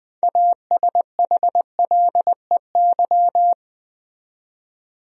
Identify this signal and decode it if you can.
Morse code, heard as a keyed tone, about 20 wpm, 700 Hz. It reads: ASHLEY
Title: USA Common Names - 20wpm